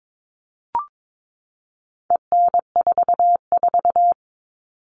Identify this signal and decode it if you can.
Morse code, heard as a keyed tone, about 22 wpm, 700 Hz. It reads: EN44
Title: All Cut Numbers - 4-Digit 22wpm